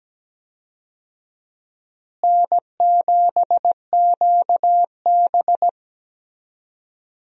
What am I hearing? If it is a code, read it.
Morse code, heard as a keyed tone, about 17 wpm, 700 Hz. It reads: N7QB
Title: USA Call Signs - 4 Characters - Repeated 3x - 17wpm